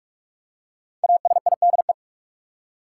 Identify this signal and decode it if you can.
Morse code, heard as a keyed tone, about 46 wpm, 700 Hz. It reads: ASIDE
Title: Five Letter Words - 45wpm